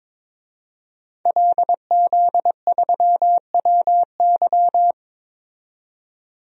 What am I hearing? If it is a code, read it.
Morse code, heard as a keyed tone, about 22 wpm, 700 Hz. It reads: LZ3WY
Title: DXCC-100 Call Signs - 5 Characters - 22wpm